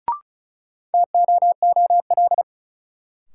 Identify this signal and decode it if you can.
Morse code, heard as a keyed tone, about 36 wpm, 700 Hz. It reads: TOOL
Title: Sentences from Top 1000 Words - Review Part 6 - New Word Drill - 35wpm